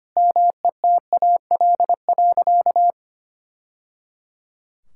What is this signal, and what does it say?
Morse code, heard as a keyed tone, about 25 wpm, 700 Hz. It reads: METAL.
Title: Sentences from Top 1000 Words - Review Part 6 - 25wpm